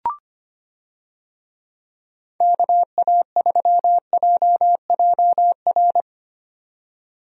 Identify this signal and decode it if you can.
Morse code, heard as a keyed tone, about 25 wpm, 700 Hz. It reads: KA3JJR
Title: USA Call Signs - 6 Characters - 25wpm